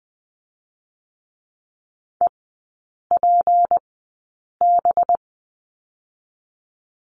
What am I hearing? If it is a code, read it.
Morse code, heard as a keyed tone, about 20 wpm, spaced as for 8 wpm, 700 Hz. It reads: EPB
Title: 184 Lesson 30e - Letter b - Three Characters 20wpm